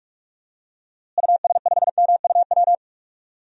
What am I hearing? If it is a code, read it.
Morse code, heard as a keyed tone, about 41 wpm, 700 Hz. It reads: US5MUW
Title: DXCC-100 Call Signs - 6 Characters - Repeated 3x - 45wpm